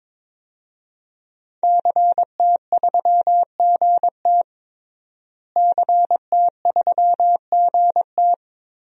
Morse code, heard as a keyed tone, about 22 wpm, 700 Hz. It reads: CT3GT CT3GT
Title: DXCC-100 Call Signs - 5 Characters - Repeated 2x - 22wpm